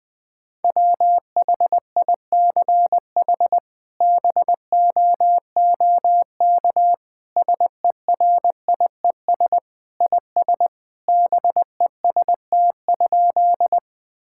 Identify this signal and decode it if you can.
Morse code, heard as a keyed tone, about 20 wpm, 700 Hz. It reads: WHICH BOOK SERIES IS BEST?
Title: Sentences from Top 1000 Words - Review Part 5 - 20wpm